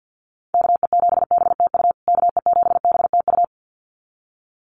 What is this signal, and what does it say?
Morse code, heard as a keyed tone, about 50 wpm, 700 Hz. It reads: KE7BTU KE7BTU
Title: USA Call Signs Repeated 2x 50wpm